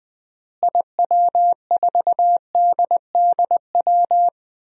Morse code, heard as a keyed tone, about 20 wpm, 700 Hz. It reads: IW4DDW
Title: DXCC-100 Call Signs - 6 Characters - Repeated 2x - 20wpm